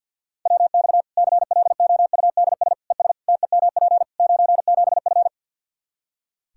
Morse code, heard as a keyed tone, about 52 wpm, 700 Hz. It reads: WX CLOUDS ES TEMP 97F
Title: Practice QSO - Exchange 2 - Info - 50wpm